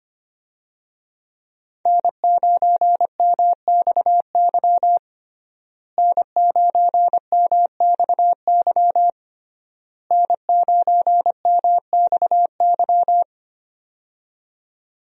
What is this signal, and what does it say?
Morse code, heard as a keyed tone, about 25 wpm, 700 Hz. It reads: N9MXY N9MXY N9MXY
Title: USA Call Signs - 5 Characters - Repeated 3x - 25wpm